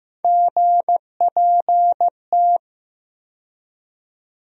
Morse code, heard as a keyed tone, about 16 wpm, 700 Hz. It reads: GPT
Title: USA Call Sign Postfixes - 3 Letters 15wpm